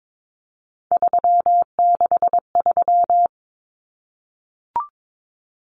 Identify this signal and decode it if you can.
Morse code, heard as a keyed tone, about 22 wpm, 700 Hz. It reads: 363
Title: Common Cut Numbers - 3-Digit 22wpm